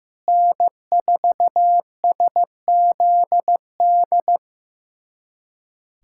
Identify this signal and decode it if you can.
Morse code, heard as a keyed tone, about 16 wpm, 700 Hz. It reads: N4SZD
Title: USA Callsign DE Callsign - 5 Characters - 15wpm